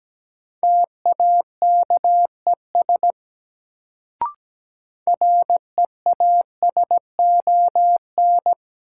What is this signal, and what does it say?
Morse code, heard as a keyed tone, about 17 wpm, 700 Hz. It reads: TAKES REASON
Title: Sets of 5 Words Encore - New Word Drill - 17wpm